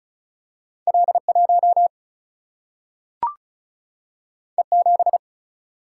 Morse code, heard as a keyed tone, about 36 wpm, 700 Hz. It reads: L1 E7
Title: Single-Letter Single-Number 35wpm